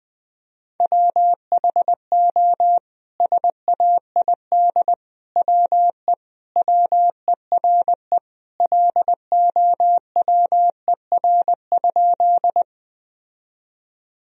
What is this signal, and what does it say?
Morse code, heard as a keyed tone, about 20 wpm, 700 Hz. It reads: WHO SAID WE WERE LOWER?
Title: Sentences from Top 1000 Words - Review Part 4 - 20wpm